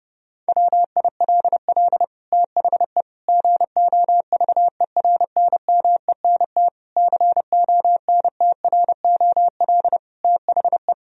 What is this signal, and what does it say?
Morse code, heard as a keyed tone, about 32 wpm, 700 Hz. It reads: WILL THE GOVERNMENT CONTROL THE
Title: Sentences from Top 1000 Words - Review Part 3 - 30wpm